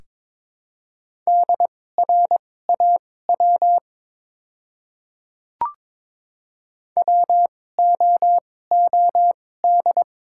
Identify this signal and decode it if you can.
Morse code, heard as a keyed tone, about 23 wpm, spaced as for 16 wpm, 700 Hz. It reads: DRAW WOOD
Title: 2x Character Spacing - Top 300 Words - 22wpm